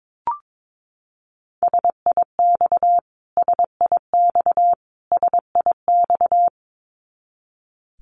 Morse code, heard as a keyed tone, about 22 wpm, 700 Hz. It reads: SIX SIX SIX
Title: Top 500 Words Repeated 22wpm